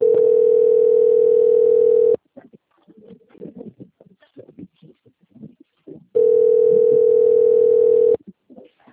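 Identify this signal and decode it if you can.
Ringback tone (North America), 440 + 480 Hz.
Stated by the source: Outgoing telephone call recorded on the Itatiba II system.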